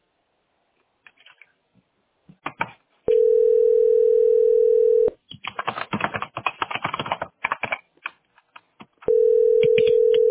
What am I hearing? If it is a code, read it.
Ringback tone (North America), 440 + 480 Hz.
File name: abacc3fb-9581-4208-907b